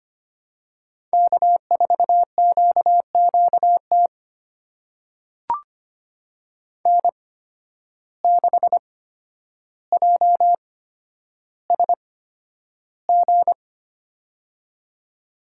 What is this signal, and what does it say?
Morse code, heard as a keyed tone, about 25 wpm, spaced as for 11 wpm, 700 Hz. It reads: K4QQT N6JSG
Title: USA Call Signs - 5 Characters - 8x Character Spacing - 25wpm